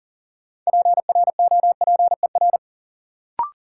Morse code, heard as a keyed tone, about 40 wpm, 700 Hz. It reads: PROPER
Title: Sentences from Top 800 Words - New Word Drill - 40wpm